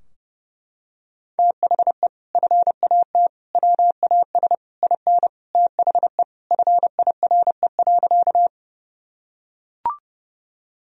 Morse code, heard as a keyed tone, about 30 wpm, 700 Hz. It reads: THE FAT WAS IN THE FIRE.
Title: Sentences from Top 1000 Words - Review Part 4 - 30wpm